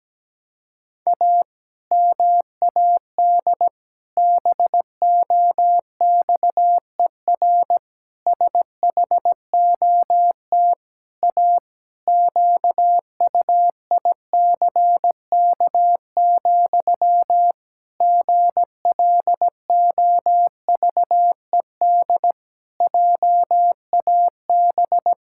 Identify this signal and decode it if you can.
Morse code, heard as a keyed tone, about 17 wpm, 700 Hz. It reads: A MAD BOXER SHOT A QUICK, GLOVED JAB
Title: Pangram Challenge v2 - 17wpm